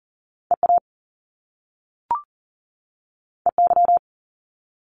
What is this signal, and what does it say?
Morse code, heard as a keyed tone, about 39 wpm, 700 Hz. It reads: EA EY
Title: Top Two Letter Combinations in Words 40wpm